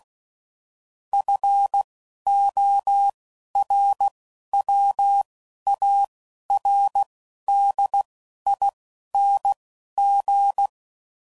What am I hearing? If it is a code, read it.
Morse code, heard as a keyed tone, about 16 wpm, spaced as for 11 wpm, 800 Hz. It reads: FORWARDING